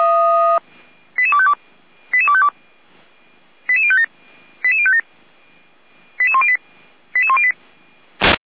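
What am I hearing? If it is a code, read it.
Five-tone selective call (ZVEI), heard as quick runs of five tones, tones of about 70 ms. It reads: 88252 88252 88057 88057 88108 88108
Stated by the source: ZVEI five-tone selective call 88102 received on SDR, on the Feuerwehr Orlamünde (f85360) audio server.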